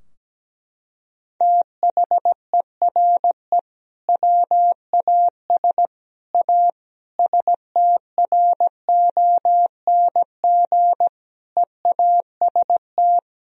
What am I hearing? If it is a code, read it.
Morse code, heard as a keyed tone, about 17 wpm, 700 Hz. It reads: THERE WAS A STRONG EAST
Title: Sentences from Top 1000 Words - Review Part 3 - 17wpm